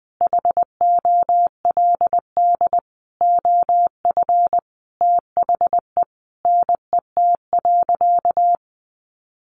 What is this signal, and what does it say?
Morse code, heard as a keyed tone, about 20 wpm, 700 Hz. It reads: HOLD OF THE NET.
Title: Sentences from Top 1500 Words 20wpm